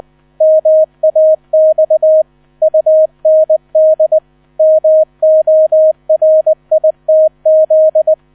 Morse code, heard as a keyed tone, about 20 wpm, 630 Hz. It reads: MAX UND MORITZ
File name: 000 Vorwort 20wpm